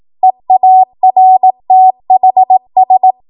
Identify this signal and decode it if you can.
Morse code, heard as a keyed tone, about 18 wpm, 750 Hz. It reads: EARTHS